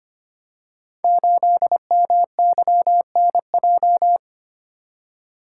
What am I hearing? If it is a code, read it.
Morse code, heard as a keyed tone, about 25 wpm, 700 Hz. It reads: 8MYNJ
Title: Code Groups - 5 Letter:Numbers - 25wpm